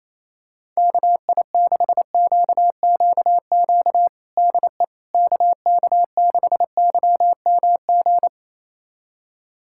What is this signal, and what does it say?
Morse code, heard as a keyed tone, about 28 wpm, 700 Hz. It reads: KI6QQQ DE KK6YMG
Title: USA Callsign DE Callsign 28wpm